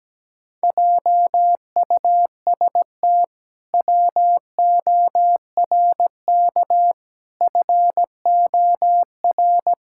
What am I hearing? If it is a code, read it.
Morse code, heard as a keyed tone, about 17 wpm, 700 Hz. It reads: JUST WORK FOR
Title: Pangram Challenge v2 - 17wpm